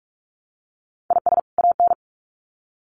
Morse code, heard as a keyed tone, about 44 wpm, 700 Hz. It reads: IS AN
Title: Sets of 2 Words 45wpm